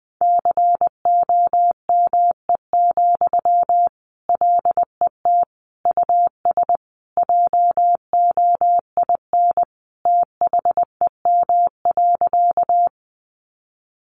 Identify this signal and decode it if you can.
Morse code, heard as a keyed tone, about 20 wpm, 700 Hz. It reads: COME, LET US JOIN THEM.
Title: Sentences from Top 1400 Words 20wpm